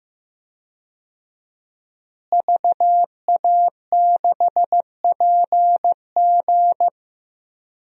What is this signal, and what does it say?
Morse code, heard as a keyed tone, about 15 wpm, 700 Hz. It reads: VA6PG
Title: DXCC-100 Call Signs - 5 Characters - Repeated 2x - 15wpm